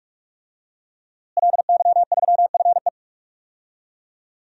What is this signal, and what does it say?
Morse code, heard as a keyed tone, about 47 wpm, 700 Hz. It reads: LY3FE
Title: DXCC-100 Call Signs - 5 Characters - Repeated 3x - 45wpm